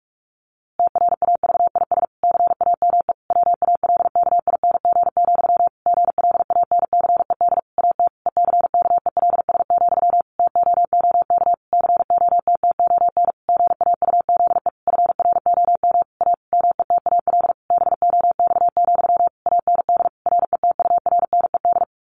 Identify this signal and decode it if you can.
Morse code, heard as a keyed tone, about 46 wpm, 700 Hz. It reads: TRAVIS CAME WALKING, GLANCED AT ECKELS, TOOK COTTON GAUZE FROM A METAL BOX, AND RETURNED
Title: A Sound Of Thunder 45wpm